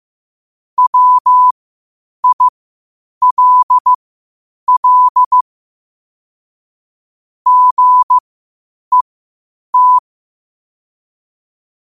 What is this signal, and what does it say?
Morse code, heard as a keyed tone, about 15 wpm, spaced as for 8 wpm, 1000 Hz. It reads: WILL GET
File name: qotdfiles / 8wpm